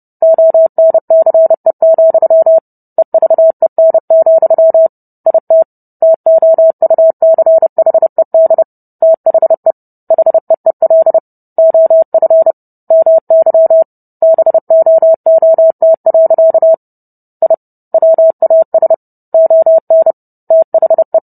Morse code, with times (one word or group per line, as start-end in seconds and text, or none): 0.2-2.6 ONCE,
3.0-4.9 EVEN,
5.3-5.6 IT
6.0-8.6 TOUCHED
9.0-9.7 THE
10.1-11.2 HEEL
11.6-12.5 OF
12.9-13.8 MY
14.2-16.7 BOOT.
17.4-17.5 I
17.9-18.9 WAS
19.3-20.1 ON
20.5-21.2 THE